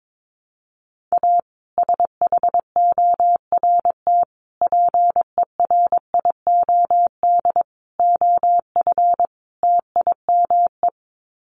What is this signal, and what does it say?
Morse code, heard as a keyed tone, about 22 wpm, 700 Hz. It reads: A SHORT PERIOD OF TIME
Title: Sets of 5 Words Encore 22wpm